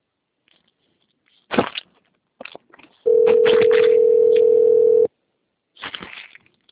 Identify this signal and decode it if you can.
Ringback tone (North America), 440 + 480 Hz.
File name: out- - - - - .26910